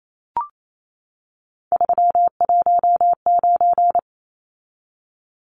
Morse code, heard as a keyed tone, about 28 wpm, 700 Hz. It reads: 319
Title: Numbers - 3-Digit 28wpm